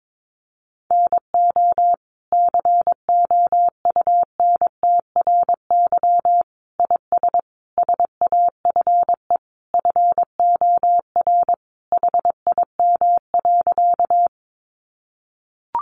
Morse code, heard as a keyed tone, about 22 wpm, 700 Hz. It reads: NO COUNTRY IS SAFE FOR HIM.
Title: Sentences from Top 900 Words 22wpm